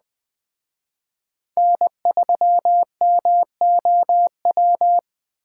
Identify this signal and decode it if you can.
Morse code, heard as a keyed tone, about 20 wpm, 700 Hz. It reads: N3MOW